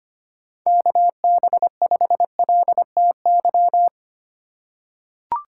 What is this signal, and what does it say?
Morse code, heard as a keyed tone, about 25 wpm, 700 Hz. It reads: KB5LTY
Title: USA Call Signs - 6 Characters - 3x Character Spacing - 25wpm